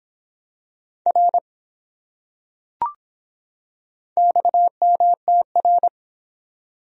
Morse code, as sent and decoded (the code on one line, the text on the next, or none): .-. / -..- -- - .-.
R XMTR